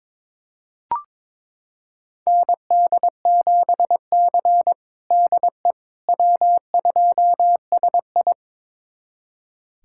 Morse code, heard as a keyed tone, about 22 wpm, 700 Hz. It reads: ND7C DE W2SI
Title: USA Callsign DE Callsign - 4 Characters - 22wpm